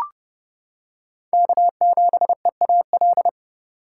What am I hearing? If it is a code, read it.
Morse code, heard as a keyed tone, about 30 wpm, 700 Hz. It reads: K7EAL